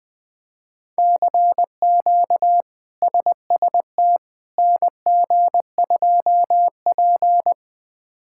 Morse code, heard as a keyed tone, about 20 wpm, 700 Hz. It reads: CQ SST NG2P
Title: K1USN SST QSO Practice - 20wpm